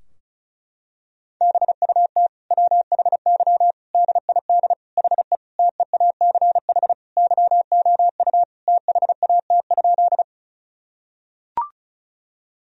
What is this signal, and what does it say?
Morse code, heard as a keyed tone, about 36 wpm, 700 Hz. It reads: BUT WHY DID HE TEACH YOU THAT?
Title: Sentences from Top 1000 Words - Review Part 2 - 35wpm